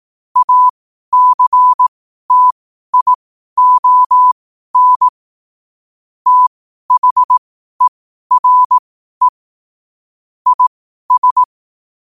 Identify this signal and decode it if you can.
Morse code, heard as a keyed tone, about 18 wpm, spaced as for 12 wpm, 1000 Hz. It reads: ACTION THERE IS
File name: qotdfiles / 12wpm